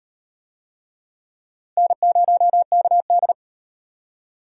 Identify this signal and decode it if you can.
Morse code, heard as a keyed tone, about 38 wpm, 700 Hz. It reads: N0KD